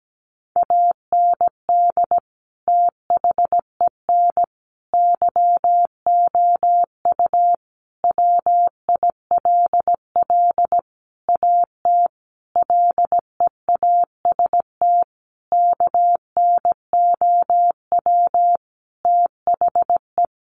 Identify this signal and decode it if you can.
Morse code, heard as a keyed tone, about 17 wpm, 700 Hz. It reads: AND THEN YOU WILL AT LEAST KNOW THE
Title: Sentences from Top 1500 Words 17wpm